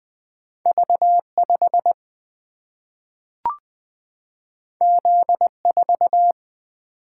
Morse code, heard as a keyed tone, about 20 wpm, 700 Hz. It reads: V5 Z4
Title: Single-Letter Single-Number 20wpm